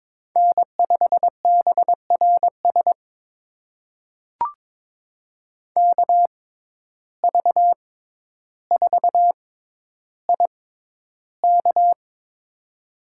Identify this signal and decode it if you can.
Morse code, heard as a keyed tone, about 22 wpm, spaced as for 11 wpm, 700 Hz. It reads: N5BRS KV4IK
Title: USA Call Signs - 5 Characters - 6x Character Spacing - 22wpm